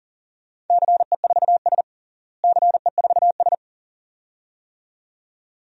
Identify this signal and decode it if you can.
Morse code, heard as a keyed tone, about 40 wpm, 700 Hz. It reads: CE4S CE4S
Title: DXCC-100 Call Signs - 4 Characters - Repeated 2x - 40wpm